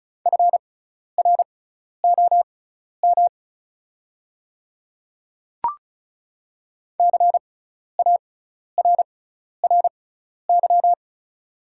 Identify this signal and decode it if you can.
Morse code, heard as a keyed tone, about 34 wpm, spaced as for 12 wpm, 700 Hz. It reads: FROM CARRY
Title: 6x Character Spacing - Top 500 Words - 35wpm